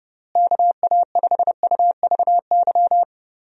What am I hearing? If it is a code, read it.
Morse code, heard as a keyed tone, about 32 wpm, 700 Hz. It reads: KA5UVY
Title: USA Call Signs - 6 Characters - Repeated 2x - 30wpm